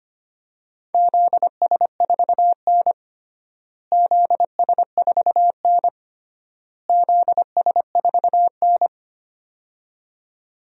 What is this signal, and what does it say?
Morse code, heard as a keyed tone, about 25 wpm, 700 Hz. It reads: ZS4N ZS4N ZS4N
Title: DXCC-100 Call Signs - 4 Characters - Repeated 3x - 25wpm